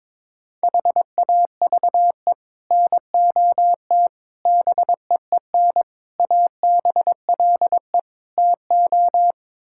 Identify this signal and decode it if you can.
Morse code, heard as a keyed tone, about 22 wpm, 700 Hz. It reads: HAVE NOT BEEN ABLE TO
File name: practice / 22wpm / Build A Phrase - 5-Word Phrases - 22wpm